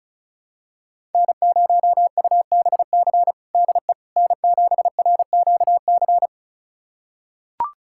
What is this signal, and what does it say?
Morse code, heard as a keyed tone, about 34 wpm, 700 Hz. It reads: N0UBC DE N7RQC
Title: USA Callsign DE Callsign - 5 Characters - 35wpm